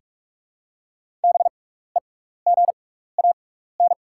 Morse code, heard as a keyed tone, about 44 wpm, spaced as for 15 wpm, 690 Hz. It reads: BEGAN
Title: 6x Character Spacing - Top 500 Words - 45wpm